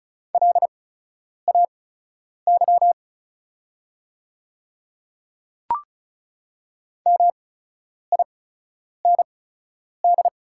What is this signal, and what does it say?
Morse code, heard as a keyed tone, about 36 wpm, spaced as for 10 wpm, 700 Hz. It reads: LAY MIND
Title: 8x Character Spacing - Top 500 Words - 35wpm